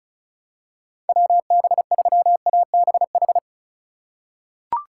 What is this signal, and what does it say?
Morse code, heard as a keyed tone, about 36 wpm, 700 Hz. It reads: WB3ABH
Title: USA Call Signs - 4x Character Spacing - 35wpm